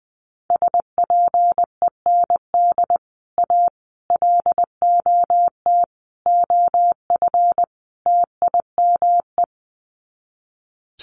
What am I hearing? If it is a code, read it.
Morse code, heard as a keyed tone, about 20 wpm, 700 Hz. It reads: SPEND A LOT OF TIME
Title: Sets of 5 Words 20wpm